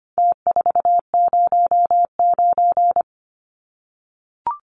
Morse code, heard as a keyed tone, about 25 wpm, 700 Hz. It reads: T409
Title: Numbers - Contest Style - 4-Digit 25wpm